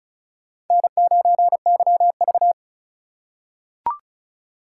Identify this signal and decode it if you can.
Morse code, heard as a keyed tone, about 36 wpm, 700 Hz. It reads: N9YV
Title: USA Call Signs - 3x Character Spacing - 35wpm